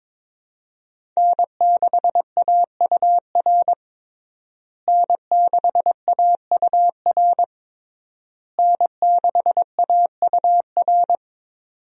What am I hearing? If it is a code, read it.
Morse code, heard as a keyed tone, about 22 wpm, 700 Hz. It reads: N6AUR N6AUR N6AUR
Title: USA Call Signs - 5 Characters - Repeated 3x - 22wpm